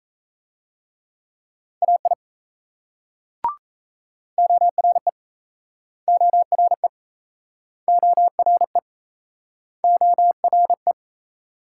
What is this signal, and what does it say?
Morse code, heard as a keyed tone, about 36 wpm, 700 Hz. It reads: AI ORE ORE ORE ORE